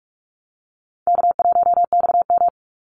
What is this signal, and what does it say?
Morse code, heard as a keyed tone, about 44 wpm, 700 Hz. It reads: K1XM
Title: Easy Call Signs Repeated 2x - 45wpm